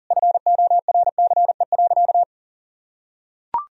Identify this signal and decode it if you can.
Morse code, heard as a keyed tone, about 40 wpm, 700 Hz. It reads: FORCE.
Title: Sentences from Top 1000 Words - Review Part 4 - 40wpm